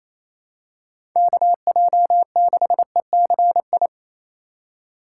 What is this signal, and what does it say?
Morse code, heard as a keyed tone, about 27 wpm, 700 Hz. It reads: KJ6ECI
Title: USA Call Signs - 6 Characters - 4x Character Spacing - 28wpm